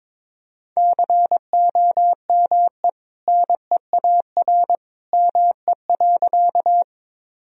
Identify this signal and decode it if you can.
Morse code, heard as a keyed tone, about 22 wpm, 700 Hz. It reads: COME NEAR ME.
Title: Sentences from Top 1000 Words - Review Part 1 - 22wpm